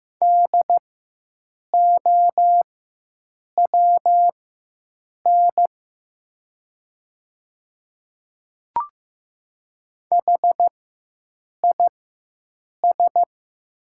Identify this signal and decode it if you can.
Morse code, heard as a keyed tone, about 15 wpm, spaced as for 7 wpm, 700 Hz. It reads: DOWN HIS
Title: 4x Character Spacing - Top 200 Words - 15wpm